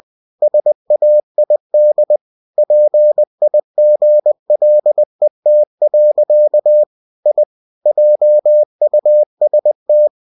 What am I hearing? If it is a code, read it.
Morse code, heard as a keyed tone, about 20 wpm, 600 Hz. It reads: SAID PIGLET. I JUST
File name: cw-books / winnie-the-pooh / winnie-20wpm-page-0048